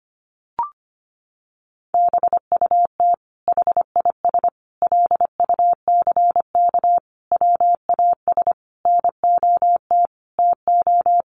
Morse code, heard as a keyed tone, about 25 wpm, 700 Hz. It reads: BUT HIS LUCK WAS NOT TO
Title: Sentences from Top 1500 Words 25wpm